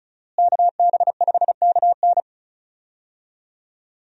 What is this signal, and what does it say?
Morse code, heard as a keyed tone, about 34 wpm, 700 Hz. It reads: KB5KN
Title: USA Call Signs - 5 Characters - Repeated 2x - 35wpm